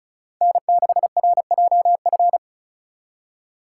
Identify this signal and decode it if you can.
Morse code, heard as a keyed tone, about 36 wpm, 700 Hz. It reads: N6RJF
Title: USA Call Signs - 5 Characters - 2x Character Spacing - 35wpm